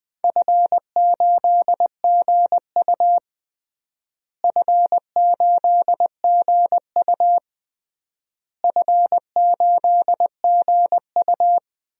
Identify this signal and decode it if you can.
Morse code, heard as a keyed tone, about 21 wpm, 700 Hz. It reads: F8GU F8GU F8GU
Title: DXCC-100 Call Signs - 4 Characters - Repeated 3x - 20wpm